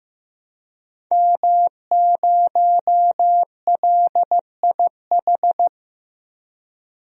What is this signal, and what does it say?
Morse code, heard as a keyed tone, about 15 wpm, 700 Hz. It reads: M0LIH
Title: DXCC-100 Call Signs - 5 Characters - Repeated 2x - 15wpm